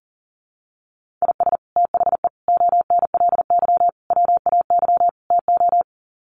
Morse code, heard as a keyed tone, about 40 wpm, 700 Hz. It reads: IS THE ONLY WAY TO
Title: Sets of 5 Words Encore 40wpm